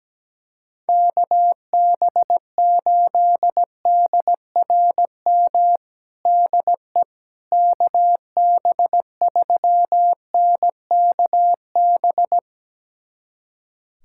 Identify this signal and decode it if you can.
Morse code, heard as a keyed tone, about 17 wpm, 700 Hz. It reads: KB8DRM DE KB3NKB
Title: USA Callsign DE Callsign - 6 Characters - 17wpm